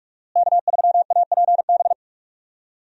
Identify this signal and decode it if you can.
Morse code, heard as a keyed tone, about 44 wpm, 700 Hz. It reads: K3APB
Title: USA Call Signs - 5 Characters - Repeated 2x - 45wpm